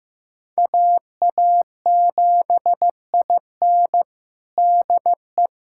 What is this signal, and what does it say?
Morse code, heard as a keyed tone, about 15 wpm, 700 Hz. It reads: AA7IN DE
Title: USA Callsign DE Callsign 15wpm